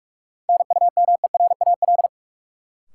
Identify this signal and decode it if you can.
Morse code, heard as a keyed tone, about 46 wpm, 700 Hz. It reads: NUMERAL
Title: Sentences from Top 1000 Words - Review Part 2 - New Word Drill - 45wpm